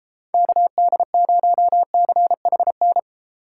Morse code, heard as a keyed tone, about 33 wpm, 700 Hz. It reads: KD0CHN